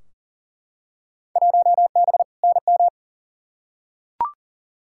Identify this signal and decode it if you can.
Morse code, heard as a keyed tone, about 40 wpm, 700 Hz. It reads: 1B NM
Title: Field Day Exchanges Encore - 40wpm